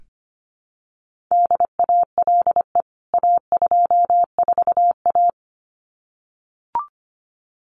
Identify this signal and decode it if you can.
Morse code, heard as a keyed tone, about 25 wpm, 700 Hz. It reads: DALE A24A
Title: CWT Exchange 25wpm